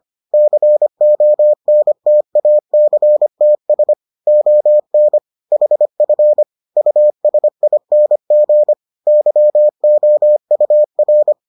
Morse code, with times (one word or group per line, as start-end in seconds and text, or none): 0.3-3.9 CONTACTS
4.3-5.2 ON
5.5-6.4 HF
6.8-8.7 USING
9.1-11.3 YOUR